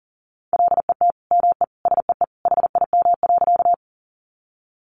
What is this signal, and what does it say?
Morse code, heard as a keyed tone, about 40 wpm, 700 Hz. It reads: LET ME SEE HIM.
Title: Sentences from Top 300 Words 40wpm